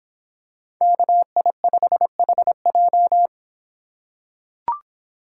Morse code, heard as a keyed tone, about 26 wpm, 700 Hz. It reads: KI5HJ